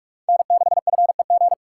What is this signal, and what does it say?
Morse code, heard as a keyed tone, about 57 wpm, 700 Hz. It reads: N6FEG